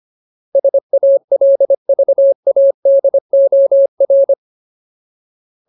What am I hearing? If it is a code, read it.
Morse code, heard as a keyed tone, about 25 wpm, 550 Hz. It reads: SALVADOR
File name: Ultimate Word List / 25WPM Ultimate